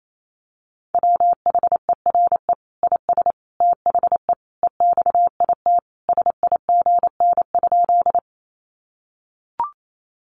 Morse code, heard as a keyed tone, about 28 wpm, 700 Hz. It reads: WHERE IS THE EXIT SIGN?
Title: Sentences from Top 1100 Words 28wpm